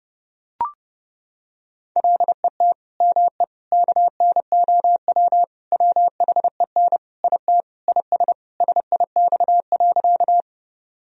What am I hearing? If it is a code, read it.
Morse code, heard as a keyed tone, about 30 wpm, 700 Hz. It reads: LET ME KNOW WHEN IT IS SIX.
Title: Sentences from Top 1000 Words - Review Part 2 - 30wpm